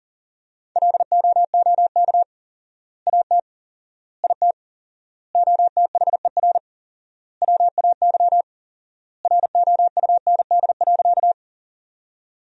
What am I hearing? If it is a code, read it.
Morse code, heard as a keyed tone, about 40 wpm, 700 Hz. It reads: LOOK AT IT OTHER WAY ROUND.
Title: 4x Word-Spacing - Sentences from Top 200 Words - 40wpm